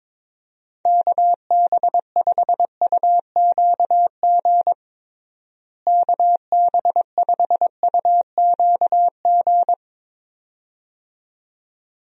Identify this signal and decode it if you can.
Morse code, heard as a keyed tone, about 22 wpm, 700 Hz. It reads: KB5UQG KB5UQG
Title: USA Call Signs - 6 Characters - Repeated 2x - 22wpm